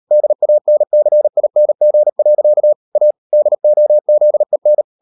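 Morse code, heard as a keyed tone, about 38 wpm, 600 Hz. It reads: DANCING. A DOZEN
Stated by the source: Morse code practice at 38 WPM, sending Ken Follett's novel Never in English, from CW med Gnister.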